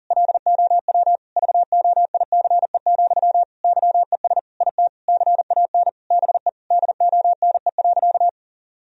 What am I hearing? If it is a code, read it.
Morse code, heard as a keyed tone, about 40 wpm, 700 Hz. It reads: LOW VOICE, YES IT CAN BE DONE.
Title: Sentences from Top 1000 Words - Review Part 1 - 40wpm